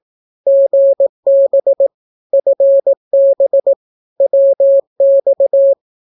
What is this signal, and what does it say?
Morse code, heard as a keyed tone, about 18 wpm, 550 Hz. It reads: GB FB WX
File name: naukapodstaw / skroty / skroty